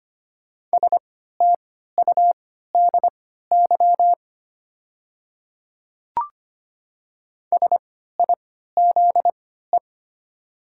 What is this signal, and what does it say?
Morse code, heard as a keyed tone, about 25 wpm, spaced as for 14 wpm, 700 Hz. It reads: STUDY SIZE
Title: 3x Character Spacing - Top 500 Words - 25wpm